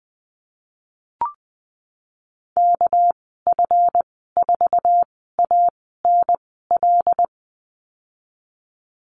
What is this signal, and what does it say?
Morse code, heard as a keyed tone, about 20 wpm, spaced as for 15 wpm, 700 Hz. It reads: KF4ANL
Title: 238 Lesson 38g - Letter q - Call Signs 20wpm